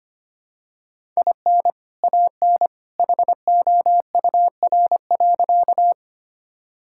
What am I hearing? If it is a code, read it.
Morse code, heard as a keyed tone, about 25 wpm, 700 Hz. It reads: IN AN HOUR.
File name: practice / 25wpm / Sentences with 2-Word Phrases - 25wpm